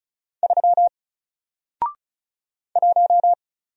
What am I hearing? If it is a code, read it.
Morse code, heard as a keyed tone, about 33 wpm, 700 Hz. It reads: 3 1
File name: Single-Letter-Number-35wpm